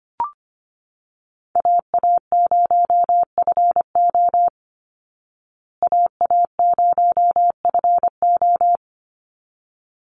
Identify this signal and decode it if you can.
Morse code, heard as a keyed tone, about 25 wpm, 700 Hz. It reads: AA0FO AA0FO
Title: Call Signs Repeated 3x - 25wpm